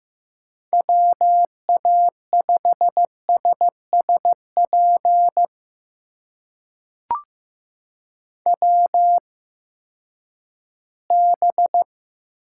Morse code, heard as a keyed tone, about 15 wpm, spaced as for 10 wpm, 700 Hz. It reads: WA5SSP WB
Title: USA Call Signs - 6 Characters - 8x Character Spacing - 15wpm